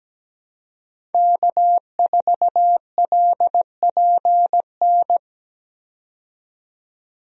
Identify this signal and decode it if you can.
Morse code, heard as a keyed tone, about 17 wpm, 700 Hz. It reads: K4LPN
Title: USA Call Signs - 5 Characters - 17wpm